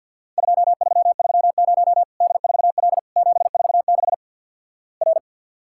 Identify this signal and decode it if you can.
Morse code, heard as a keyed tone, about 49 wpm, 700 Hz. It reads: 2330 D4L 746 R
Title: Worked All Europe DX Contest - QTC Exchange with Repeats - 50wpm